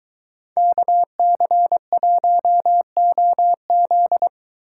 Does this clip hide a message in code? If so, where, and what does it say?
Yes: Morse code from 0.6 to 4.3 s, heard as a keyed tone, about 22 wpm, 700 Hz. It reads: KC1OZ